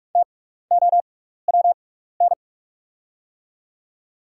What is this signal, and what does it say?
Morse code, heard as a keyed tone, about 44 wpm, spaced as for 15 wpm, 690 Hz. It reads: TOWN
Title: 6x Character Spacing - Top 500 Words - 45wpm